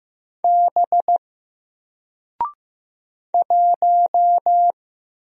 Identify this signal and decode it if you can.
Morse code, heard as a keyed tone, about 15 wpm, 700 Hz. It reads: B 1
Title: Single Letter-Number 15wpm